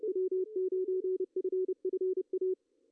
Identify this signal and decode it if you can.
Morse code, heard as a keyed tone, about 30 wpm, 370 Hz, with radio noise. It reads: W9FFA
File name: CW / cqww-cw-k5tr